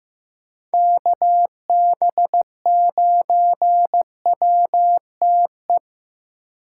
Morse code, heard as a keyed tone, about 15 wpm, 700 Hz. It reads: KB9WTE
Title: USA Call Signs - 6 Characters - 2x Character Spacing - 15wpm